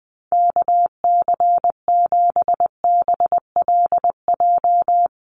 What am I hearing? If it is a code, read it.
Morse code, heard as a keyed tone, about 20 wpm, 700 Hz. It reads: KC7BLJ
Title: USA Call Signs 20wpm